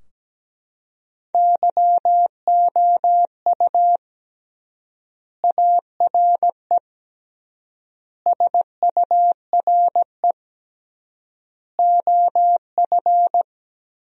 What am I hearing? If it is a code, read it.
Morse code, heard as a keyed tone, about 17 wpm, 700 Hz. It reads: YOU ARE SURE OF
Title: 3x Word-Spacing - Sentences from Top 300 Words - 17wpm